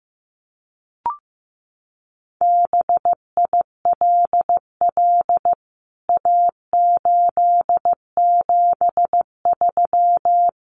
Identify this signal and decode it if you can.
Morse code, heard as a keyed tone, about 15 wpm, 700 Hz. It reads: BILL A873
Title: CWT Exchange 15wpm